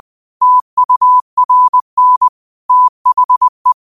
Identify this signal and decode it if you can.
Morse code, heard as a keyed tone, about 20 wpm, 1000 Hz. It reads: TURN THE
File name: qotdfiles / 20wpm